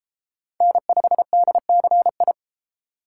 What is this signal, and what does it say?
Morse code, heard as a keyed tone, about 30 wpm, 700 Hz. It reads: N5DCI